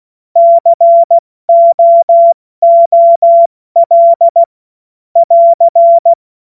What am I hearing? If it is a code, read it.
Morse code, heard as a keyed tone, about 16 wpm, spaced as for 14 wpm, 680 Hz. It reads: COOL <AR>
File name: Morsekurs / Witze / Witz 05. April 2022 70 F80